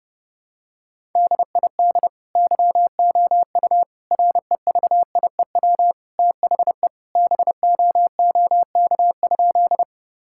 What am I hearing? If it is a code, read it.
Morse code, heard as a keyed tone, about 30 wpm, 700 Hz. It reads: DID YOU REVIEW THE BOOK?
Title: Sentences from Top 1000 Words - Review Part 6 - 30wpm